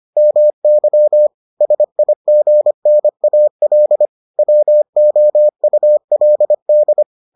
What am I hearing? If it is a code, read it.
Morse code, heard as a keyed tone, about 25 wpm, 600 Hz. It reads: MY SIGNAL WOULD